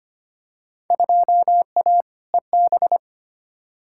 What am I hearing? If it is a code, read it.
Morse code, heard as a keyed tone, about 25 wpm, 700 Hz. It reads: 2A EB
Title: Field Day Exchanges - 25wpm